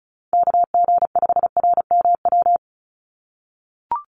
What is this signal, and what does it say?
Morse code, heard as a keyed tone, about 34 wpm, 700 Hz. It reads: KG5RMW
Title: USA Call Signs 35wpm